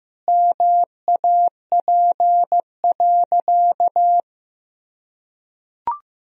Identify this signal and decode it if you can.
Morse code, heard as a keyed tone, about 15 wpm, 700 Hz. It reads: MAP.
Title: Sentences from Top 1000 Words - Review Part 2 - 15wpm